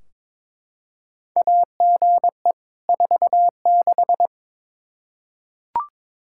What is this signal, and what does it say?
Morse code, heard as a keyed tone, about 21 wpm, 700 Hz. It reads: AGE 46
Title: Practice QSO - Exchange 3 - More Info - 22wpm